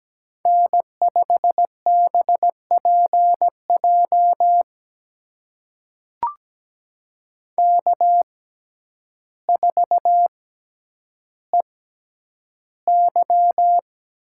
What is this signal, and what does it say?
Morse code, heard as a keyed tone, about 17 wpm, spaced as for 9 wpm, 700 Hz. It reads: N5BPJ K4EY
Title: USA Call Signs - 5 Characters - 6x Character Spacing - 17wpm